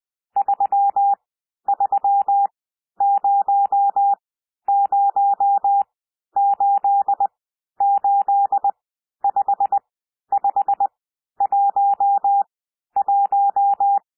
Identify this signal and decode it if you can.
Morse code, heard as a keyed tone, about 20 wpm, spaced as for 11 wpm, 800 Hz. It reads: 3300885511
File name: cwmp3 / N1 SOUND NUMBERS 20 wpm